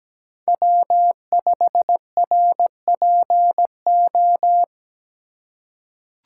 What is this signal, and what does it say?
Morse code, heard as a keyed tone, about 17 wpm, 700 Hz. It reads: W5RPO
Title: USA Call Signs - 5 Characters - 3x Character Spacing - 17wpm